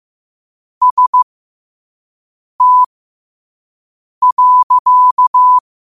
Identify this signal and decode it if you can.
Morse code, heard as a keyed tone, about 15 wpm, spaced as for 5 wpm, 1000 Hz. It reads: ST.
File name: qotdfiles / 5wpm